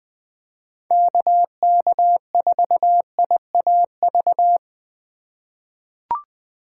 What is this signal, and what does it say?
Morse code, heard as a keyed tone, about 20 wpm, 700 Hz. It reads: KK4IAV
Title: USA Call Signs - 6 Characters - 3x Character Spacing - 20wpm